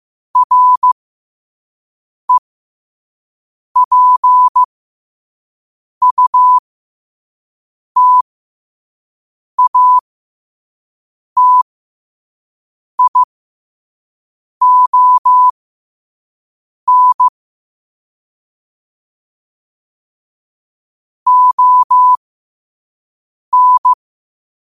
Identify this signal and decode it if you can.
Morse code, heard as a keyed tone, about 15 wpm, spaced as for 5 wpm, 1000 Hz. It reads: REPUTATION ON
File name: qotdfiles / 5wpm